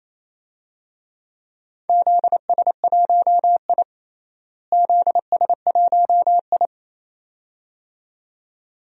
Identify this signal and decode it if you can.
Morse code, heard as a keyed tone, about 28 wpm, 700 Hz. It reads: ZS1I ZS1I
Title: DXCC-100 Call Signs - 4 Characters - Repeated 2x - 28wpm